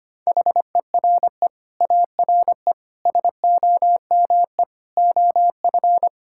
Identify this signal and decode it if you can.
Morse code, heard as a keyed tone, about 25 wpm, 700 Hz. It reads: HERE ARE SOME OF
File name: Sets of 9 Words - 25wpm